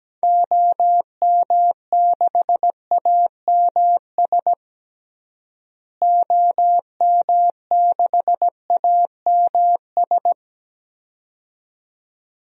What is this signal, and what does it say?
Morse code, heard as a keyed tone, about 17 wpm, 700 Hz. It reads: OM6AMS OM6AMS
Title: DXCC-100 Call Signs - 6 Characters - Repeated 2x - 17wpm